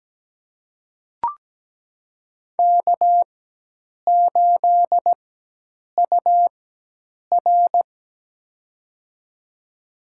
Morse code, heard as a keyed tone, about 17 wpm, spaced as for 8 wpm, 700 Hz. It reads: K8UR
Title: USA Call Signs - 4x Character Spacing - 17wpm